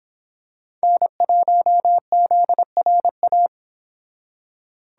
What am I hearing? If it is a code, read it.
Morse code, heard as a keyed tone, about 26 wpm, 700 Hz. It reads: N1ZRA